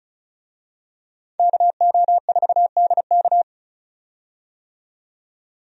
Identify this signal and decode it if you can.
Morse code, heard as a keyed tone, about 36 wpm, 700 Hz. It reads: KO4DK
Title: USA Call Signs - 5 Characters - 35wpm